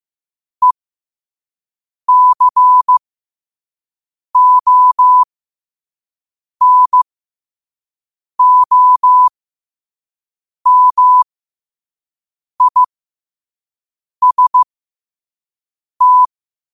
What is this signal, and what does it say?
Morse code, heard as a keyed tone, about 15 wpm, spaced as for 5 wpm, 1000 Hz. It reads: ECONOMIST
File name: qotdfiles / 5wpm